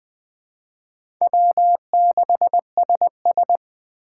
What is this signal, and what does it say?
Morse code, heard as a keyed tone, about 20 wpm, 700 Hz. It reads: W6SS
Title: Speed Racing - USA 1x2 Call Signs 17wpm